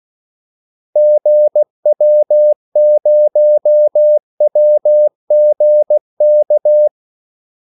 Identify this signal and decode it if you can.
Morse code, heard as a keyed tone, about 16 wpm, 600 Hz. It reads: GW0WGK
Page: Kaldesignaler 16 wpm